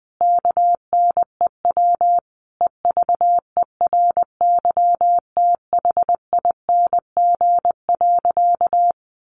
Morse code, heard as a keyed tone, about 20 wpm, 700 Hz. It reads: KNEW EVERYTHING.
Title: Sentences from Top 700 Words 20wpm